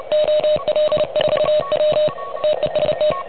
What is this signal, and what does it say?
Morse code, heard as a keyed tone, about 30 wpm, 590 Hz. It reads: OL4W TEST
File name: truSDX / MWC29.8